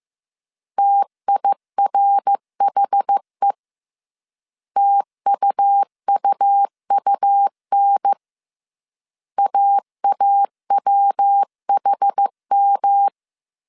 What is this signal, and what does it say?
Morse code, heard as a keyed tone, about 15 wpm, 800 Hz. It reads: TIRHE TUUUN AAWHM